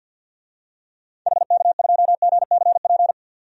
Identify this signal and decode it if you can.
Morse code, heard as a keyed tone, about 47 wpm, 700 Hz. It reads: HK2GCP
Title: DXCC-100 Call Signs - 6 Characters - 50wpm